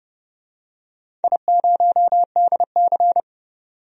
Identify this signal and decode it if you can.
Morse code, heard as a keyed tone, about 30 wpm, 700 Hz. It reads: I0DC
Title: DXCC-100 Call Signs - 4 Characters - 30wpm